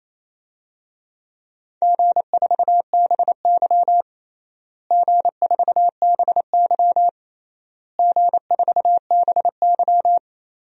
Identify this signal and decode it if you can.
Morse code, heard as a keyed tone, about 27 wpm, 700 Hz. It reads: G4BY G4BY G4BY
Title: DXCC-100 Call Signs - 4 Characters - Repeated 3x - 28wpm